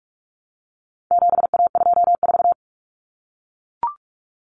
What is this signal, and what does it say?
Morse code, heard as a keyed tone, about 44 wpm, 700 Hz. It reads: 7A24
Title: Common Cut Numbers - 4-Digit 45wpm